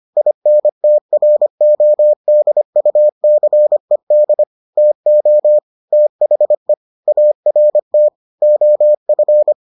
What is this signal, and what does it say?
Morse code, heard as a keyed tone, about 25 wpm, 600 Hz. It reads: INTRODUCED TO THE ART OF